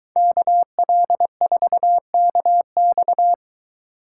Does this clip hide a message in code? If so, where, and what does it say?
Yes: Morse code from 0.2 to 3.3 s, heard as a keyed tone, about 22 wpm, 700 Hz. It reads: KL4KX